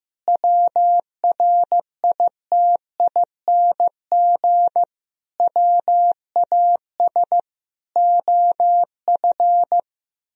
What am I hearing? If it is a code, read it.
Morse code, heard as a keyed tone, about 15 wpm, 700 Hz. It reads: WRITING WAS OF
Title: Sentences from Top 1000 Words - Review Part 3 - 15wpm